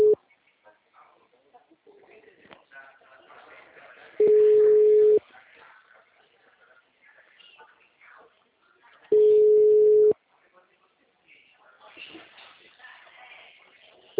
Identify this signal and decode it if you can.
Ringback tone (Europe, 425 Hz), with speech.
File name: out- -2002- - - .92345